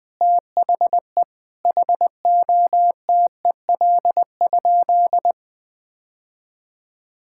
Morse code, heard as a keyed tone, about 21 wpm, 700 Hz. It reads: THE HOTEL?
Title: Sentences from Top 1600 Words 20wpm